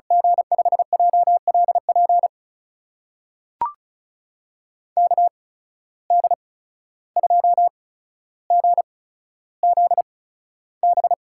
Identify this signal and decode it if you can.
Morse code, heard as a keyed tone, about 36 wpm, spaced as for 14 wpm, 700 Hz. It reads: G5JLP KD2GZB
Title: USA Call Signs - 6 Characters - 8x Character Spacing - 35wpm